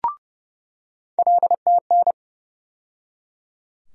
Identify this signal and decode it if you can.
Morse code, heard as a keyed tone, about 30 wpm, 700 Hz. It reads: LTN
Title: USA Call Sign Postfixes - 3 Letters 30wpm